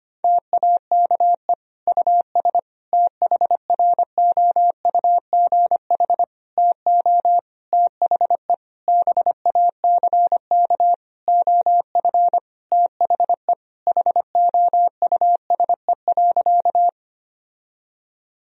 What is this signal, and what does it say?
Morse code, heard as a keyed tone, about 25 wpm, 700 Hz. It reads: TAKE US THROUGH TO THE BACK OF THE HOUSE.
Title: Sentences from Top 1000 Words - Review Part 1 - 25wpm